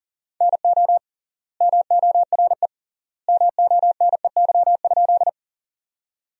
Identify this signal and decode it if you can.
Morse code, heard as a keyed tone, about 37 wpm, 700 Hz. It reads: NO MORE MONEY?
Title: 3x Word-Spacing - Sentences from Top 400 Words - 40wpm